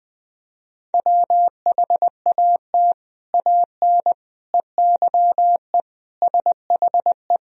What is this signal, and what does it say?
Morse code, heard as a keyed tone, about 20 wpm, 700 Hz. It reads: WHAT AN EYE SHE
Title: Sentences from Top 1000 Words - Review Part 1 - 20wpm